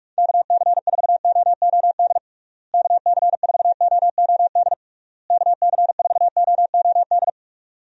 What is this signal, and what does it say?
Morse code, heard as a keyed tone, about 46 wpm, 700 Hz. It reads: KC4OOD KC4OOD KC4OOD
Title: USA Call Signs - 6 Characters - Repeated 3x - 45wpm